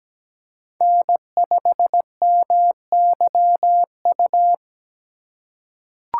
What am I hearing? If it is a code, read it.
Morse code, heard as a keyed tone, about 17 wpm, 700 Hz. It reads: N5MYU
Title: USA Call Signs - 5 Characters - 3x Character Spacing - 17wpm